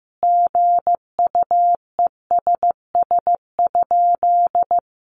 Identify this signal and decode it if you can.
Morse code, heard as a keyed tone, about 15 wpm, 700 Hz. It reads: GUESS?
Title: Sentences from Top 1500 Words 15wpm